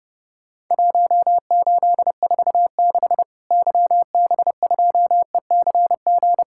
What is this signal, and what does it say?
Morse code, heard as a keyed tone, about 30 wpm, 700 Hz. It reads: 1846 YB2ECG
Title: Worked All Europe DX Contest - QTC Exchange - 30wpm